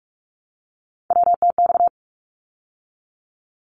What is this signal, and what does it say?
Morse code, heard as a keyed tone, about 48 wpm, 700 Hz. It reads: WTX
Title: ARRL Sections 45wpm